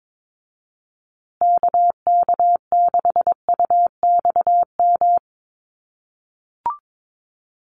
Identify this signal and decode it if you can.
Morse code, heard as a keyed tone, about 22 wpm, 700 Hz. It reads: KK6UXM
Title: USA Call Signs Repeated 2x 22wpm